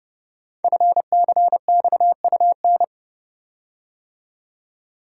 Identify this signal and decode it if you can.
Morse code, heard as a keyed tone, about 30 wpm, 700 Hz. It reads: FCXUN
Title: Code Groups - 5 Letters - 30wpm